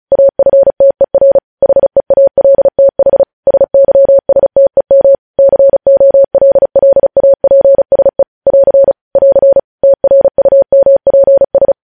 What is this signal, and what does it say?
Morse code, heard as a keyed tone, about 36 wpm, 550 Hz. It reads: AFTER HEALTH SYSTEM COLLAPSE <AR> <AR> TRUMPS
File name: podcast / News headlines-1x-35x35